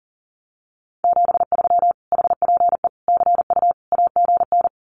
Morse code, heard as a keyed tone, about 40 wpm, 700 Hz. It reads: 73 HPE CU AGN
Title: Common QSO Phrases Encore 40wpm